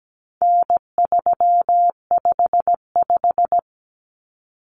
Morse code, heard as a keyed tone, about 17 wpm, 700 Hz. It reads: N355
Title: Common Cut Numbers - 4-Digit 17wpm